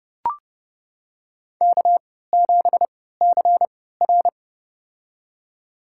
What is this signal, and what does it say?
Morse code, heard as a keyed tone, about 30 wpm, spaced as for 17 wpm, 700 Hz. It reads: K7CR
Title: USA Call Signs - 4 Characters - 3x Character Spacing - 30wpm